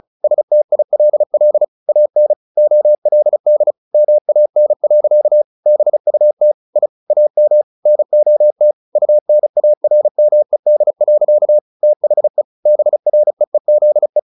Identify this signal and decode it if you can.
Morse code, heard as a keyed tone, about 34 wpm, 600 Hz. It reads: STILL AN OLD MAN. BUT I AM NOT UNARMED. THE BREEZE
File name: the old man and the sea 2 35wpm